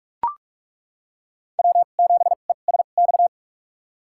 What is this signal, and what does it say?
Morse code, heard as a keyed tone, about 41 wpm, spaced as for 31 wpm, 700 Hz. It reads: W7ESX
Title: USA Call Signs - 5 Characters - 2x Character Spacing - 45wpm